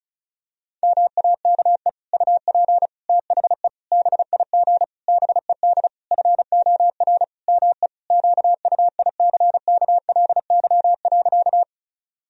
Morse code, heard as a keyed tone, about 34 wpm, 700 Hz. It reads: MAKE UP THE BIG BED FOR ME QUICKLY.
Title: Sentences from Top 1000 Words - Review Part 1 - 35wpm